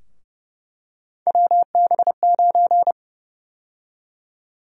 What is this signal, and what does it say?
Morse code, heard as a keyed tone, about 31 wpm, 700 Hz. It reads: WB9
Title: USA Call Sign Prefixes - Repeated 2x - 30wpm